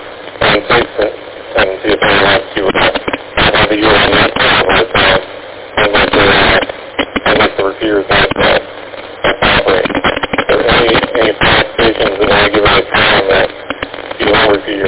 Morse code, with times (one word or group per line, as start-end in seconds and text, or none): none